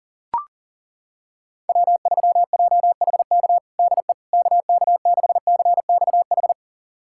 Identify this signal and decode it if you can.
Morse code, heard as a keyed tone, about 40 wpm, 700 Hz. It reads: W3JHK DE KK6CXH
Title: USA Callsign DE Callsign 40wpm